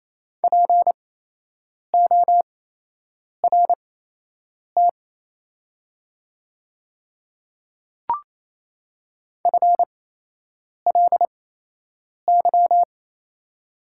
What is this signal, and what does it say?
Morse code, heard as a keyed tone, about 28 wpm, spaced as for 8 wpm, 700 Hz. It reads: PORT FLY
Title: 8x Character Spacing - Top 500 Words - 28wpm